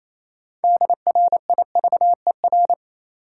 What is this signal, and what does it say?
Morse code, heard as a keyed tone, about 28 wpm, 700 Hz. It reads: DRIVER
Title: Sentences from Top 1000 Words - New Word Drill - 28wpm